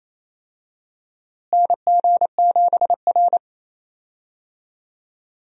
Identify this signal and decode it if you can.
Morse code, heard as a keyed tone, about 28 wpm, 700 Hz. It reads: NG7R
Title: USA Call Signs - 4 Characters - Repeated 2x - 28wpm